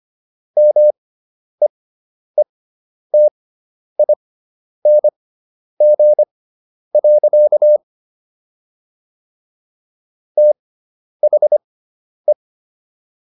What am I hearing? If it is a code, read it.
Morse code, heard as a keyed tone, about 25 wpm, spaced as for 10 wpm, 600 Hz. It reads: MEETING. THE